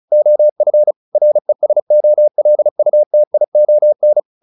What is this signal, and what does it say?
Morse code, heard as a keyed tone, about 34 wpm, 600 Hz. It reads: OF RESOLUTION
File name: the old man and the sea 2 35wpm